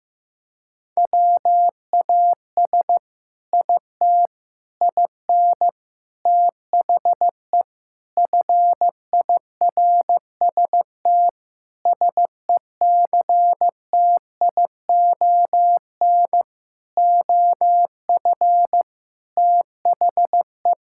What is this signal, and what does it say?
Morse code, heard as a keyed tone, about 16 wpm, 700 Hz. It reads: WAS IT IN THE FIRST SECTION OF THE
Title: Sentences from Top 1000 Words - Review Part 5 - 15wpm